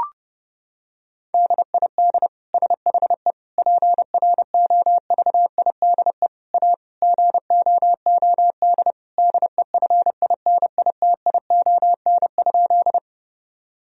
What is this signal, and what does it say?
Morse code, heard as a keyed tone, about 32 wpm, 700 Hz. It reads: DID SHE PROVIDE A GOOD DEFINITION?
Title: Sentences from Top 1000 Words - Review Part 4 - 30wpm